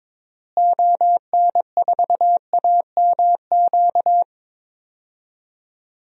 Morse code, heard as a keyed tone, about 23 wpm, 700 Hz. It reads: ON4AMQ
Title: DXCC-100 Call Signs - 6 Characters - Repeated 2x - 22wpm